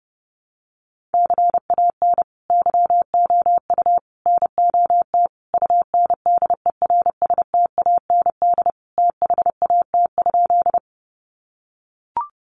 Morse code, heard as a keyed tone, about 30 wpm, 700 Hz. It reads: CAN YOU NOT UNDERSTAND THAT?
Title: Sentences from Top 600 Words 30wpm